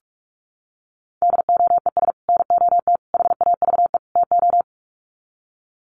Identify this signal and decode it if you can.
Morse code, heard as a keyed tone, about 44 wpm, 700 Hz. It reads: DOES NOT HAVE TO
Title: Sets of 4 Words 45wpm